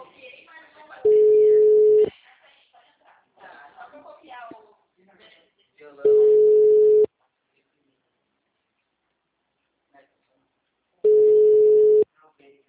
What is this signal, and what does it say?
Ringback tone (Europe, 425 Hz), with speech.